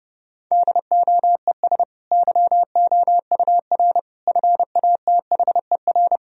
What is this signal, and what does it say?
Morse code, heard as a keyed tone, about 32 wpm, 700 Hz. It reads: DOES YOUR FATHER
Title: Sentences from Top 1000 Words - Review Part 2 - 30wpm